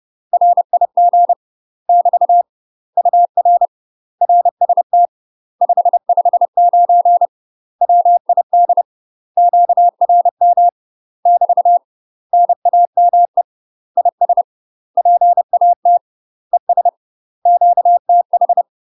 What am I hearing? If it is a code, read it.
Morse code, heard as a keyed tone, about 30 wpm, 700 Hz. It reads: RIG <BT> UR RST 559 WID QRM <BT> NAME IS PAT ES QTH
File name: UK MASTER QSOs TRAINING / 30WPM / UK MASTER QSOs 0048